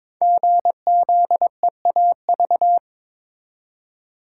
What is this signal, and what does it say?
Morse code, heard as a keyed tone, about 23 wpm, 700 Hz. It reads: GZEAV
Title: Code Groups - 5 Letter:Numbers - 22wpm